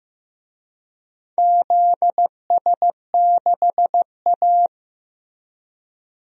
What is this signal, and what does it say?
Morse code, heard as a keyed tone, about 15 wpm, 700 Hz. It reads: ZS6A
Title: DXCC-100 Call Signs - 4 Characters - Repeated 3x - 15wpm